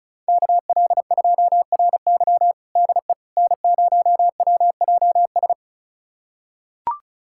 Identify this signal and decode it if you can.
Morse code, heard as a keyed tone, about 36 wpm, 700 Hz. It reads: KL2RY DE N0WJS
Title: USA Callsign DE Callsign - 5 Characters - 35wpm